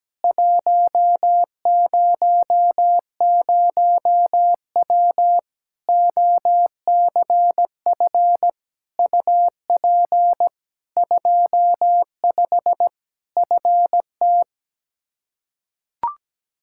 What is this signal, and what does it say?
Morse code, heard as a keyed tone, about 17 wpm, 700 Hz. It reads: 100W OCF UP 25 FT
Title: Practice QSO - Exchange 2 - Info - 17wpm